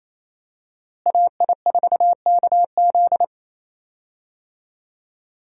Morse code, heard as a keyed tone, about 28 wpm, 700 Hz. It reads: AI4KZ
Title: USA Call Signs - 5 Characters - Repeated 3x - 28wpm